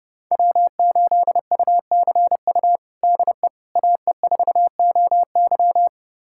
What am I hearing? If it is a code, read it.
Morse code, heard as a keyed tone, about 30 wpm, 700 Hz. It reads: W8UCU DE AE4OY
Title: USA Callsign DE Callsign 30wpm